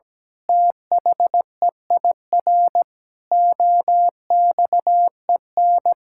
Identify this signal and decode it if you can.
Morse code, heard as a keyed tone, about 17 wpm, 700 Hz. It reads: THEIR OXEN
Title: Pangram Challenge v2 - 17wpm